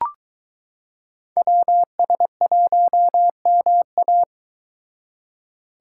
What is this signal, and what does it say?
Morse code, heard as a keyed tone, about 22 wpm, 700 Hz. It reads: WS1MA